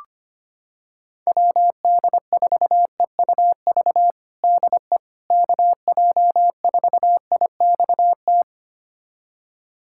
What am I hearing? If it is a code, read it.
Morse code, heard as a keyed tone, about 25 wpm, 700 Hz. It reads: WD4EUV DE KJ4IXT
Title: USA Callsign DE Callsign - 6 Characters - 25wpm